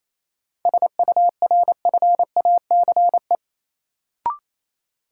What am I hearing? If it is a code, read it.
Morse code, heard as a keyed tone, about 28 wpm, 700 Hz. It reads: SURFACE
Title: Sentences from Top 600 Words - New Word Drill - 28wpm